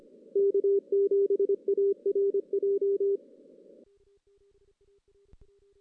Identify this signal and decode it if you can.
Morse code, heard as a keyed tone, about 25 wpm, 400 Hz. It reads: K7ARJ
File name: 2016 JAN CW NAQP / 2016 Jan naqp k5ot - 0-5